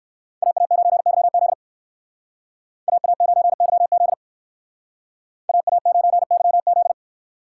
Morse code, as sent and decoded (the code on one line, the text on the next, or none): .- .- ----. -.-- --.. / .- .- ----. -.-- --.. / .- .- ----. -.-- --..
AA9YZ AA9YZ AA9YZ